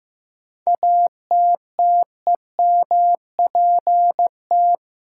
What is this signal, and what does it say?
Morse code, heard as a keyed tone, about 15 wpm, 700 Hz. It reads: ATTEMPT
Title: Sentences from Top 1000 Words - New Word Drill - 15wpm